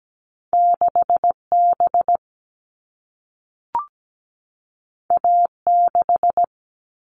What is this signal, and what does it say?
Morse code, heard as a keyed tone, about 17 wpm, 700 Hz. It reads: 6B A6
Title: All Cut Numbers - 2-Digit 17wpm